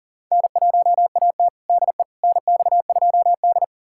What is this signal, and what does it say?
Morse code, heard as a keyed tone, about 40 wpm, 700 Hz. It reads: N1AT DE NX2D
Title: USA Callsign DE Callsign - 4 Characters - 40wpm